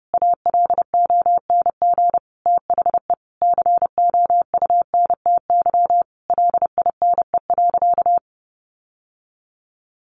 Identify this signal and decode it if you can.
Morse code, heard as a keyed tone, about 30 wpm, 700 Hz. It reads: ALONG THE COUNTY LINE.
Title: Sentences from Top 1200 Words 30wpm